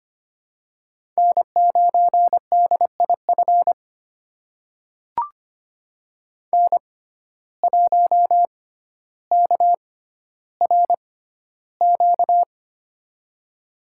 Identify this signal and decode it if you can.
Morse code, heard as a keyed tone, about 25 wpm, spaced as for 13 wpm, 700 Hz. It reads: N9DIF N1KRQ
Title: USA Call Signs - 5 Characters - 6x Character Spacing - 25wpm